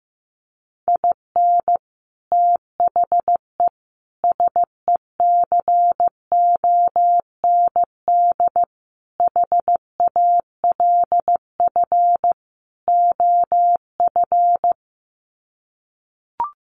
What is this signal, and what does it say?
Morse code, heard as a keyed tone, about 15 wpm, 700 Hz. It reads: IN THE SECOND HALF OF
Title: Sets of 5 Words 15wpm